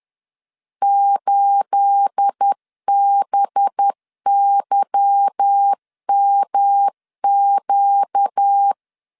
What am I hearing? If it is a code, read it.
Morse code, heard as a keyed tone, about 10 wpm, 800 Hz. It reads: 8BYMQ
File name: LEZIONI / 10 SIGNS / 10C DOT 10wpm 2940